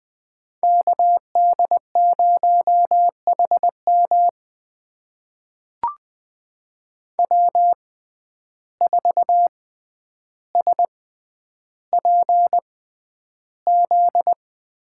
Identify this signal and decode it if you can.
Morse code, heard as a keyed tone, about 20 wpm, spaced as for 10 wpm, 700 Hz. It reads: KD0HM W4SPZ
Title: USA Call Signs - 5 Characters - 6x Character Spacing - 20wpm